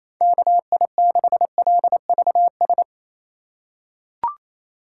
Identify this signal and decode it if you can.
Morse code, heard as a keyed tone, about 27 wpm, 700 Hz. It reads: KI6LVS
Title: USA Call Signs - 6 Characters - 3x Character Spacing - 28wpm